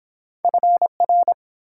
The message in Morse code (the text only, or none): FR